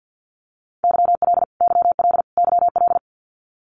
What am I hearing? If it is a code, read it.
Morse code, heard as a keyed tone, about 50 wpm, 700 Hz. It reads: YL YL YL
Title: Abbreviations Repeated 3x - 50wpm